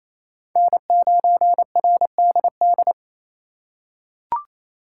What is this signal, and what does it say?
Morse code, heard as a keyed tone, about 28 wpm, 700 Hz. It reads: N9RDD
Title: USA Call Signs - 5 Characters - 2x Character Spacing - 28wpm